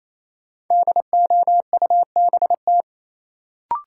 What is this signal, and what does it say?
Morse code, heard as a keyed tone, about 28 wpm, 700 Hz. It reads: DOUBT
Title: Sets of 5 Words - New Word Drill - 28wpm